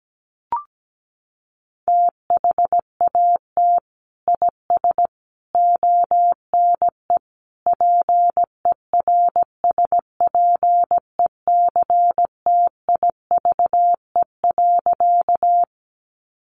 Morse code, heard as a keyed tone, about 17 wpm, 700 Hz. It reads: THAT IS ONE PERSPECTIVE.
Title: Sentences from Top 1200 Words 17wpm